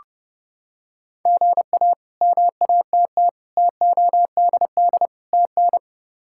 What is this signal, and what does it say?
Morse code, heard as a keyed tone, about 29 wpm, 700 Hz. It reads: GA MATT TODD TN
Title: K1USN SST QSO Practice - 30wpm